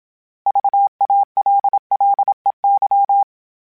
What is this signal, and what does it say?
Morse code, heard as a keyed tone, about 27 wpm, 800 Hz. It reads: VALLEY
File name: fromSMs / FromSMs mp3s / Cities@25